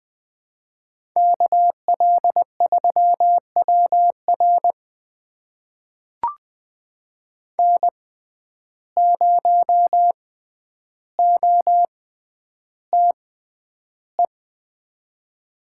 Morse code, heard as a keyed tone, about 20 wpm, spaced as for 10 wpm, 700 Hz. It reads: KL3WR N0OTE
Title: USA Call Signs - 5 Characters - 6x Character Spacing - 20wpm